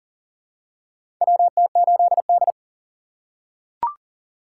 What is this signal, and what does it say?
Morse code, heard as a keyed tone, about 38 wpm, 700 Hz. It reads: WT8D
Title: USA Call Signs - 4 Characters - 3x Character Spacing - 40wpm